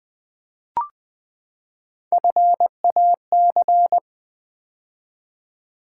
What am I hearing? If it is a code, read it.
Morse code, heard as a keyed tone, about 20 wpm, 700 Hz. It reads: FAC
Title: Build A Word - 3 Letters At A Time - 6-Letter Words - 20wpm